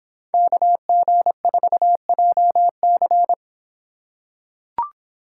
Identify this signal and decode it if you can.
Morse code, heard as a keyed tone, about 26 wpm, 700 Hz. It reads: KG4JC